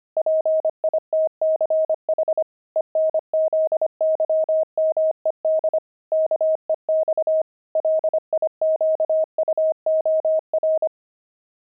Morse code, heard as a keyed tone, about 25 wpm, 630 Hz. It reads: PITCH ENZYMED KEX LIQUOR
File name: pangram-fs-25-cs-25-fuji